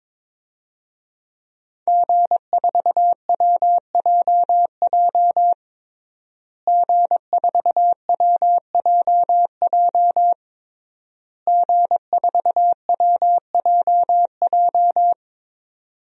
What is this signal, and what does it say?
Morse code, heard as a keyed tone, about 22 wpm, 700 Hz. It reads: G4WJJ G4WJJ G4WJJ
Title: DXCC-100 Call Signs - 5 Characters - Repeated 3x - 22wpm